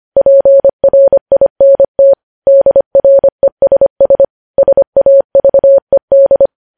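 Morse code, heard as a keyed tone, about 25 wpm, 550 Hz. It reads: PRINT DRESS SAVED
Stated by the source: Morse code news headlines at 25 wpm.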